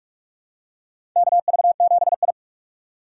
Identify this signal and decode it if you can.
Morse code, heard as a keyed tone, about 41 wpm, 700 Hz. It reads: KV7I
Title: USA Call Signs - 4 Characters - 4x Character Spacing - 45wpm